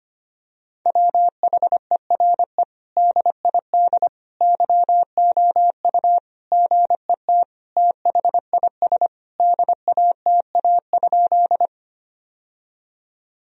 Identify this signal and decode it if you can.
Morse code, heard as a keyed tone, about 25 wpm, 700 Hz. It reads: WHERE DID YOU GET THIS DATA?
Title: Sentences from Top 1000 Words - Review Part 3 - 25wpm